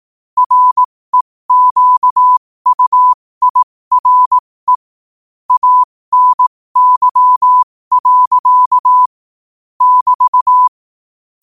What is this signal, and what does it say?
Morse code, heard as a keyed tone, about 18 wpm, spaced as for 15 wpm, 1000 Hz. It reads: REQUIRE ANY. <BT>
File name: qotdfiles / 15wpm